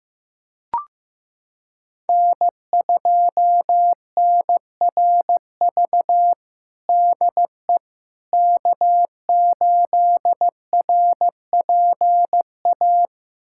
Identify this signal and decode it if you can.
Morse code, heard as a keyed tone, about 15 wpm, 700 Hz. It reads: N2NRV DE K8RPA
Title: USA Callsign DE Callsign - 5 Characters - 15wpm